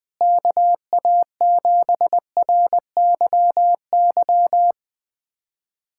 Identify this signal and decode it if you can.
Morse code, heard as a keyed tone, about 21 wpm, 700 Hz. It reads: KA7RYY
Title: USA Callsign DE Callsign - 6 Characters - 20wpm